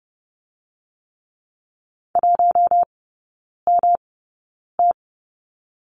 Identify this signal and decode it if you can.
Morse code, heard as a keyed tone, about 30 wpm, spaced as for 9 wpm, 700 Hz. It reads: 1MT
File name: Lesson 19e - Letter w - Three Characters 30wpm